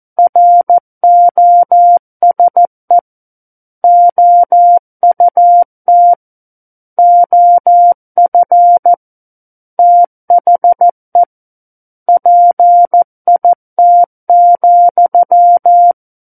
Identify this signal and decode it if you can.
Morse code, heard as a keyed tone, about 14 wpm, 700 Hz. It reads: ROSE OUT OF THE PIT,
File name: Book One / 05-Chapter 5 - 14 WPM